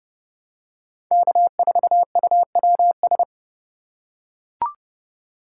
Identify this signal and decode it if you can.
Morse code, heard as a keyed tone, about 32 wpm, 700 Hz. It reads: K4UWS
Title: USA Call Signs - 5 Characters - 4x Character Spacing - 30wpm